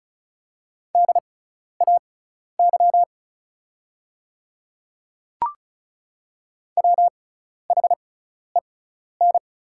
Morse code, heard as a keyed tone, about 36 wpm, spaced as for 12 wpm, 690 Hz. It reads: DAY WHEN
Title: 6x Character Spacing - Top 100 Words - 35wpm